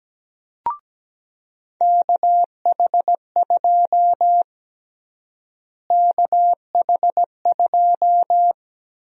Morse code, heard as a keyed tone, about 17 wpm, 700 Hz. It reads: KH2 KH2
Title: USA Call Sign Prefixes - Repeated 2x - 17wpm